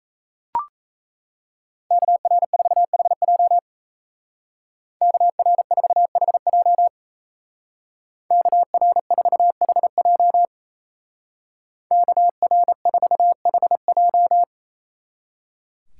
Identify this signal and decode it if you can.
Morse code, heard as a keyed tone, about 34 wpm, 700 Hz. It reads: KR4HJ KR4HJ KR4HJ KR4HJ
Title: Speed Racing - USA 2x2 Call Signs 28wpm